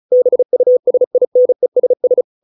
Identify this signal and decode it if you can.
Morse code, heard as a keyed tone, about 34 wpm, 500 Hz. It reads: BUSINESS